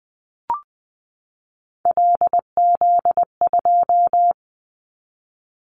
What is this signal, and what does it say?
Morse code, heard as a keyed tone, about 20 wpm, 700 Hz. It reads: LZ2
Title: Callsign Prefixes for Top 25 DXCC - 20wpm